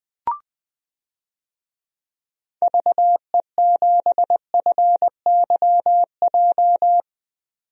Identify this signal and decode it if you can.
Morse code, heard as a keyed tone, about 20 wpm, 700 Hz. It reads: VE7FYJ
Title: DXCC-100 Call Signs - 6 Characters - 20wpm